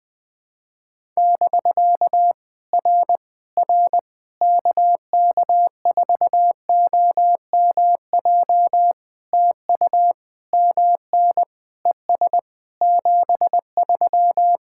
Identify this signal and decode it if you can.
Morse code, heard as a keyed tone, about 20 wpm, 700 Hz. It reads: <BK> R R KK4OMJ TU MN ES 73
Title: POTA QSO Practice - 20wpm